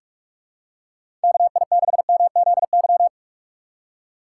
Morse code, heard as a keyed tone, about 44 wpm, 700 Hz. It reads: KI6MZY
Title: USA Call Signs - 6 Characters - 6x Character Spacing - 45wpm